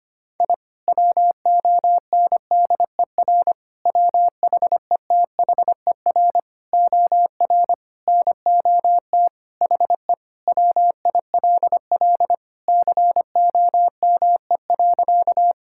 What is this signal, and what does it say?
Morse code, heard as a keyed tone, about 25 wpm, 700 Hz. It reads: I WONDER WHETHER OR NOT HE WILL COME.
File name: practice / 25wpm / Sentences with 3-Word Phrases - 25wpm